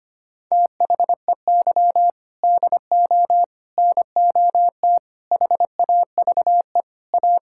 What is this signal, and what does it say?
Morse code, heard as a keyed tone, about 25 wpm, 700 Hz. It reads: THEY DO NOT HAVE A
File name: practice / 25wpm / Sentences with 4-Word Phrases - 25wpm